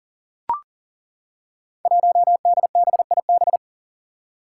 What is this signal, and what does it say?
Morse code, heard as a keyed tone, about 40 wpm, 700 Hz. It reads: 1DBIB
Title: Five Challenging Letter-Number Combinations 40wpm